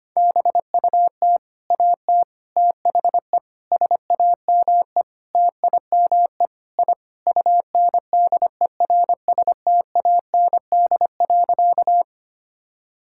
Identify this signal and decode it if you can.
Morse code, heard as a keyed tone, about 25 wpm, 700 Hz. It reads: BUT AT THE SAME TIME I UNDERSTAND.
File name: practice / 25wpm / Sentences with 4-Word Phrases - 25wpm